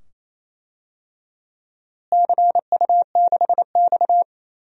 Morse code, heard as a keyed tone, about 28 wpm, 700 Hz. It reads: CU6X
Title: DXCC-100 Call Signs - 4 Characters - Repeated 2x - 28wpm